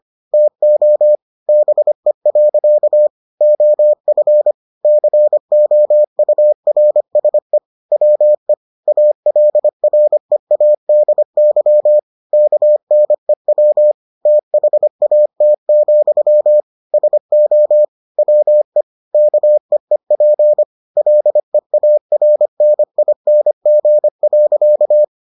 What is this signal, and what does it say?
Morse code, heard as a keyed tone, about 25 wpm, 600 Hz. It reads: TO BE. OF COURSE WE ALREADY KNEW THAT, SO WE KEEP LEARNING.